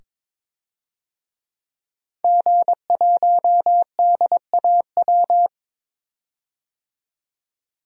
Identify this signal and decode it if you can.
Morse code, heard as a keyed tone, about 22 wpm, 700 Hz. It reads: G1DAW
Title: DXCC-100 Call Signs - 5 Characters - 22wpm